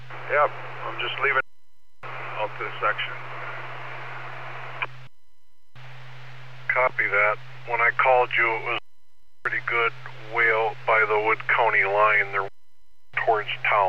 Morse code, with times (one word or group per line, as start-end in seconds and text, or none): none